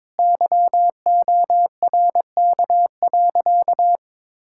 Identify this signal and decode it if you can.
Morse code, heard as a keyed tone, about 22 wpm, 700 Hz. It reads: YORK.
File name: practice / 22wpm / Sentences with 3-Word Phrases - 22wpm